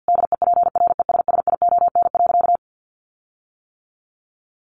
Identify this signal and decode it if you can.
Morse code, heard as a keyed tone, about 50 wpm, 700 Hz. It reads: DEPRESSION.
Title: Sentences from Top 1300 Words 50wpm